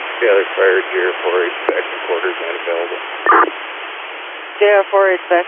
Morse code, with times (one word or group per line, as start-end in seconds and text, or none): none